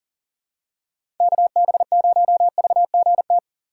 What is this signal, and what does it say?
Morse code, heard as a keyed tone, about 40 wpm, 700 Hz. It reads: KB0VGT
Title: USA Call Signs - 4x Character Spacing - 40wpm